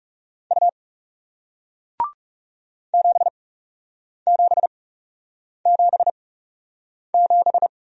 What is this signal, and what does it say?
Morse code, heard as a keyed tone, about 38 wpm, spaced as for 8 wpm, 700 Hz. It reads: U 7777